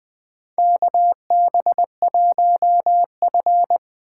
Morse code, heard as a keyed tone, about 20 wpm, 700 Hz. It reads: KB1F
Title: USA Call Signs - 4 Characters - 4x Character Spacing - 20wpm